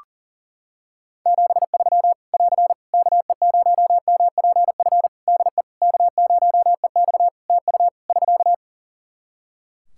Morse code, heard as a keyed tone, about 40 wpm, 700 Hz. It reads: 73 <AR> KE0MPF DE K0EX TU <SK>
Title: Practice QSO - Exchange 4 - Ending - 40wpm